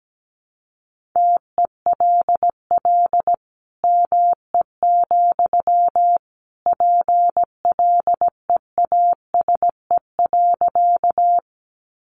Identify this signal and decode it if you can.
Morse code, heard as a keyed tone, about 17 wpm, 700 Hz. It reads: TELL ME, PLEASE.
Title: Sentences from Top 1500 Words 17wpm